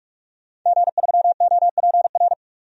Morse code, heard as a keyed tone, about 47 wpm, 700 Hz. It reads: G3OPR
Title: DXCC-100 Call Signs - 5 Characters - 45wpm